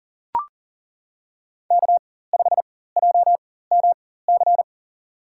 Morse code, heard as a keyed tone, about 40 wpm, spaced as for 19 wpm, 700 Hz. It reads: K5JMC
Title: USA Call Signs - 5 Characters - 4x Character Spacing - 40wpm